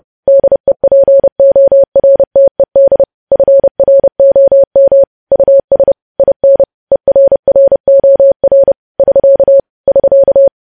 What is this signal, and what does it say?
Morse code, heard as a keyed tone, about 30 wpm, 550 Hz. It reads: DEPORTED FROM US IN ERROR <SK> <SK>
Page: news headlines in morse code at 30 wpm